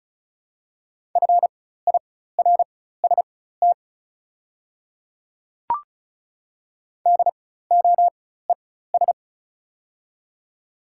Morse code, heard as a keyed tone, about 36 wpm, spaced as for 16 wpm, 700 Hz. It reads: FIRST DOES
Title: 4x Character Spacing - Top 300 Words - 35wpm